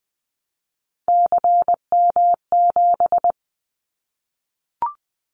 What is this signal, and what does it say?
Morse code, heard as a keyed tone, about 20 wpm, 700 Hz. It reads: CM7
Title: Callsign Prefixes for Top 100 DXCC - 20wpm